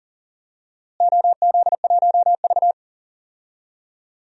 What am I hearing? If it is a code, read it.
Morse code, heard as a keyed tone, about 40 wpm, 700 Hz. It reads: OZ1V
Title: DXCC-100 Call Signs - 4 Characters - Repeated 2x - 40wpm